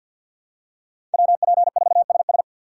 Morse code, heard as a keyed tone, about 54 wpm, 700 Hz. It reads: WP4SS
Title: USA Call Signs - 5 Characters - 8x Character Spacing - 50wpm